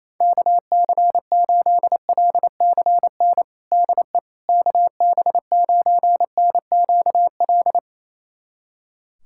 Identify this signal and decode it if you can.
Morse code, heard as a keyed tone, about 28 wpm, 700 Hz. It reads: KC8LCN DE KB9NQL
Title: USA Callsign DE Callsign - 6 Characters - 28wpm